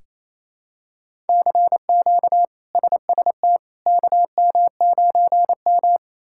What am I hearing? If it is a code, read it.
Morse code, heard as a keyed tone, about 28 wpm, 700 Hz. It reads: CQ SST KM9M
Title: K1USN SST QSO Practice - 28wpm